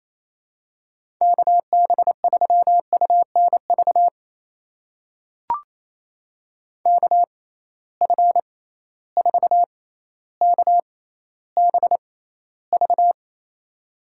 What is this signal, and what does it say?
Morse code, heard as a keyed tone, about 28 wpm, spaced as for 14 wpm, 700 Hz. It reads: KB3UNV KF4KBV
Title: USA Call Signs - 6 Characters - 6x Character Spacing - 28wpm